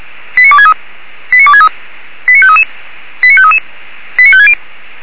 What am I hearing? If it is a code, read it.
Five-tone selective call (ZVEI), heard as quick runs of five tones, tones of about 70 ms. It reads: 89263 89263 89533 89533 89560
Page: /464/SDR/ZVEI/